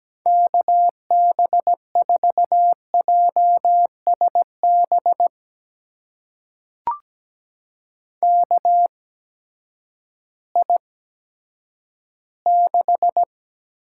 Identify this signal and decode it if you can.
Morse code, heard as a keyed tone, about 17 wpm, spaced as for 10 wpm, 700 Hz. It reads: KB4JSB KI6
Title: USA Call Signs - 6 Characters - 8x Character Spacing - 17wpm